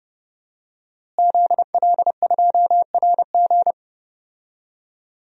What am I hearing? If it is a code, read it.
Morse code, heard as a keyed tone, about 30 wpm, 700 Hz. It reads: ZL2RG
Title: DXCC-100 Call Signs - 5 Characters - Repeated 3x - 30wpm